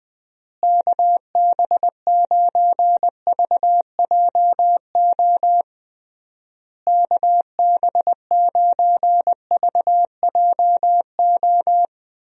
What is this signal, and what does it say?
Morse code, heard as a keyed tone, about 21 wpm, 700 Hz. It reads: KB9VJO KB9VJO
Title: USA Call Signs - 6 Characters - Repeated 3x - 20wpm